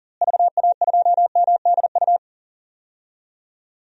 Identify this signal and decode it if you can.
Morse code, heard as a keyed tone, about 37 wpm, 700 Hz. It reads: VA2MDU
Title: DXCC-100 Call Signs - 6 Characters - Repeated 3x - 40wpm